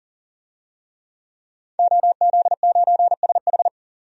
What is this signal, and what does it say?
Morse code, heard as a keyed tone, about 40 wpm, 700 Hz. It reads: OZ9SH
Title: DXCC-100 Call Signs - 5 Characters - 40wpm